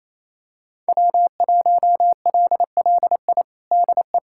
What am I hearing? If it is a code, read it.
Morse code, heard as a keyed tone, about 27 wpm, 700 Hz. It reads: W1LLI DE
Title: USA Callsign DE Callsign - 5 Characters - 28wpm